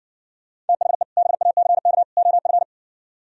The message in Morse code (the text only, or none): THE BACK OF